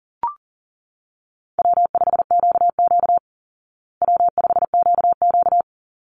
Morse code, heard as a keyed tone, about 40 wpm, 700 Hz. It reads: W5QQ W5QQ
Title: Call Signs Repeated 3x - 40wpm